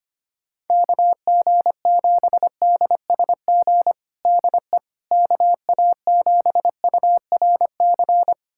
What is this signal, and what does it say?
Morse code, heard as a keyed tone, about 25 wpm, 700 Hz. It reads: KG7DSG DE KA7URC
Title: USA Callsign DE Callsign - 6 Characters - 25wpm